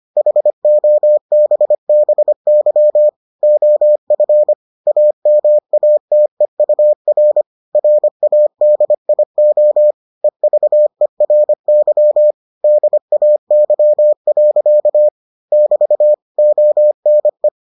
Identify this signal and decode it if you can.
Morse code, heard as a keyed tone, about 25 wpm, 600 Hz. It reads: HOBBY OF AMATEUR RADIO EVERY DAY. <BT> ONE